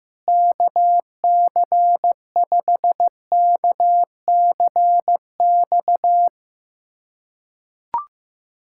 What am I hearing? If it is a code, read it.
Morse code, heard as a keyed tone, about 15 wpm, 700 Hz. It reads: KC5KCX
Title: USA Callsign DE Callsign - 6 Characters - 15wpm